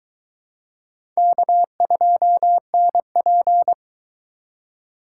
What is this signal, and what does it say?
Morse code, heard as a keyed tone, about 21 wpm, 700 Hz. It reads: K2NP